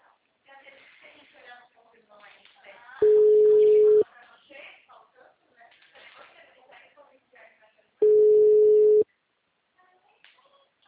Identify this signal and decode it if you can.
Ringback tone (Europe, 425 Hz), with speech.